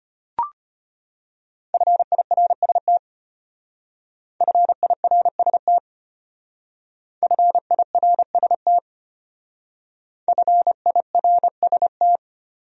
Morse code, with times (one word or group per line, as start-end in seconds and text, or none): 1.7-3.0 FIRST
4.4-5.8 FIRST
7.2-8.8 FIRST
10.3-12.2 FIRST